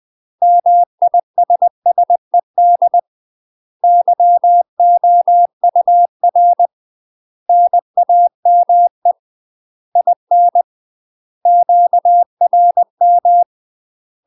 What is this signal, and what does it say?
Morse code, heard as a keyed tone, about 20 wpm, 700 Hz. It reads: MISSED YOUR NAME IN QRM
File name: UK MASTER QSOs TRAINING / 20WPM / UK MASTER QSOs 0035